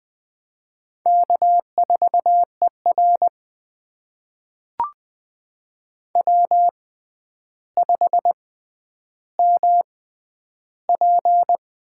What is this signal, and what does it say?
Morse code, heard as a keyed tone, about 20 wpm, spaced as for 10 wpm, 700 Hz. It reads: K4ER W5MP
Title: USA Call Signs - 4 Characters - 6x Character Spacing - 20wpm